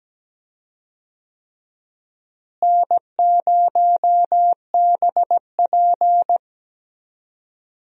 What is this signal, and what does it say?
Morse code, heard as a keyed tone, about 17 wpm, 700 Hz. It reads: N0BP
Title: USA Call Signs - 4 Characters - Repeated 3x - 17wpm